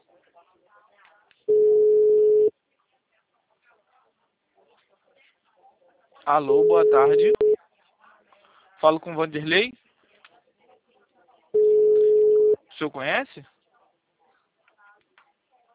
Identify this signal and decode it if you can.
Ringback tone (Europe, 425 Hz), with speech.
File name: out- -2009- - - .88281